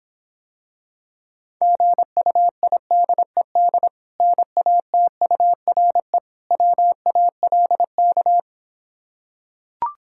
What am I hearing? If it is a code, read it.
Morse code, heard as a keyed tone, about 26 wpm, 700 Hz. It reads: GUIDED NATURE WALK